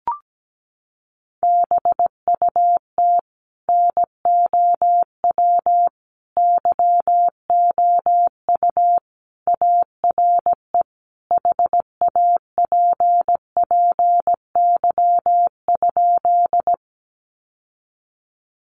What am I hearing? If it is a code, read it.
Morse code, heard as a keyed tone, about 17 wpm, 700 Hz. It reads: BUT NOW YOU ARE HAPPY?
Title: Sentences from Top 1000 Words 17wpm